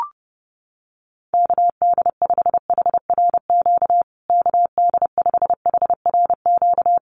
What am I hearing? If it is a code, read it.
Morse code, heard as a keyed tone, about 30 wpm, 700 Hz. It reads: KD5HRQ KD5HRQ
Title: USA Call Signs Repeated 3x 30wpm